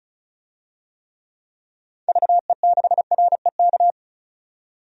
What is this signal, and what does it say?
Morse code, heard as a keyed tone, about 34 wpm, 700 Hz. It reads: VE6REK
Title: DXCC-100 Call Signs - 6 Characters - 35wpm